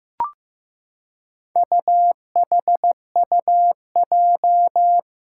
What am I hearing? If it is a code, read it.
Morse code, heard as a keyed tone, about 15 wpm, 700 Hz. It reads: UHUJ
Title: Four Challenging Letter-Number Combinations 15wpm